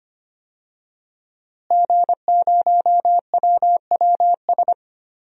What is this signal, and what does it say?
Morse code, heard as a keyed tone, about 25 wpm, 700 Hz. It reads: G0WWS
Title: DXCC-100 Call Signs - 5 Characters - Repeated 2x - 25wpm